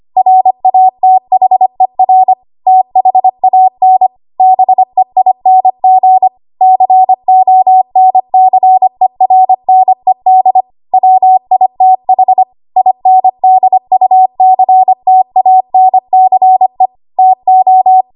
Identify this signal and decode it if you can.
Morse code, heard as a keyed tone, about 25 wpm, 750 Hz. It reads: RATHER THAN BEING CONCERNED WITH INDUCTANCE TO
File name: Archive / 25 WPM / 25WPM